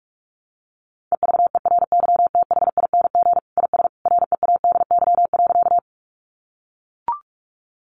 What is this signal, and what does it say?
Morse code, heard as a keyed tone, about 44 wpm, 700 Hz. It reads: EVERYTHING IS READY.
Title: Sentences from Top 700 Words 45wpm